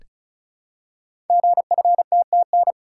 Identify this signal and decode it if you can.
Morse code, heard as a keyed tone, about 35 wpm, 700 Hz. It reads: GFTTN
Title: Code Groups - 5 Letters - 35wpm